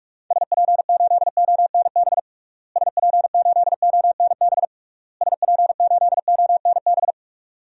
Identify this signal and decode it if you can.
Morse code, heard as a keyed tone, about 41 wpm, 700 Hz. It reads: SP8ONB SP8ONB SP8ONB
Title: DXCC-100 Call Signs - 6 Characters - Repeated 3x - 45wpm